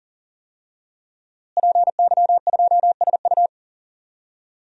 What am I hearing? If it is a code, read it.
Morse code, heard as a keyed tone, about 40 wpm, 700 Hz. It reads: PY2SU
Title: DXCC-100 Call Signs - 5 Characters - Repeated 2x - 40wpm